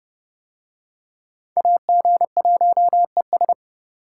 Morse code, heard as a keyed tone, about 28 wpm, 700 Hz. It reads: AG1ES